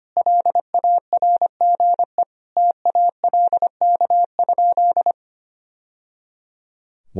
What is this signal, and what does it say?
Morse code, heard as a keyed tone, about 25 wpm, 700 Hz. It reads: LARGE TALK?
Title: Sentences from Top 1000 Words - Review Part 1 - 25wpm